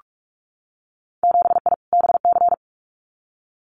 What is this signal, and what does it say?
Morse code, heard as a keyed tone, about 44 wpm, 700 Hz. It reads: 7I BC
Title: Winter Field Day Exchanges Encore 45wpm